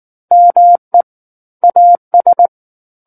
Morse code, heard as a keyed tone, about 19 wpm, 700 Hz. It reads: ME AS
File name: Book One / 10-Chapter 10 - 19 WPM